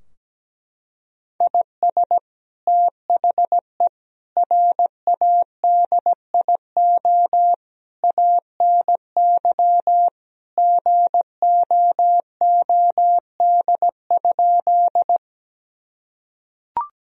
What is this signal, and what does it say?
Morse code, heard as a keyed tone, about 17 wpm, 700 Hz. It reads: IS THE RADIO ANY GOOD?
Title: Sentences from Top 1000 Words - Review Part 3 - 17wpm